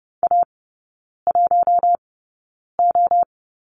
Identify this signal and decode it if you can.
Morse code, heard as a keyed tone, about 30 wpm, spaced as for 9 wpm, 700 Hz. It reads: A1O
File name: course / 30:10wpm / 087 Lesson 15e - Number 5 - Three Characters 30wpm